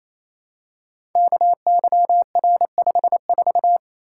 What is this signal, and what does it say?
Morse code, heard as a keyed tone, about 27 wpm, 700 Hz. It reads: KYR54
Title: Code Groups - 5 Letter:Numbers - 28wpm